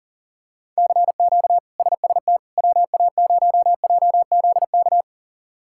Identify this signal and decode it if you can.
Morse code, heard as a keyed tone, about 37 wpm, 700 Hz. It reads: CQ SST WA0JZK
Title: K1USN SST QSO Practice - 40wpm